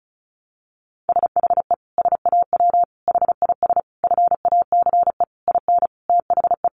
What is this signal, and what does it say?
Morse code, heard as a keyed tone, about 34 wpm, 700 Hz. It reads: SHE SAW HIS FACE IN THE
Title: Sentences from Top 1300 Words 35wpm